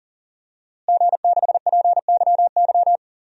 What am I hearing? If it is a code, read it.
Morse code, heard as a keyed tone, about 40 wpm, 700 Hz. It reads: G6PYY
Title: DXCC-100 Call Signs - 5 Characters - Repeated 3x - 40wpm